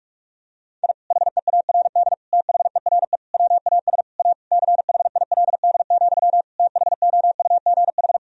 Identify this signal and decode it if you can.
Morse code, heard as a keyed tone, about 41 wpm, 700 Hz. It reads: I HEARD THERE WAS A CHILD, THOUGH
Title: Sentences from Top 1000 Words - Review Part 2 - 45wpm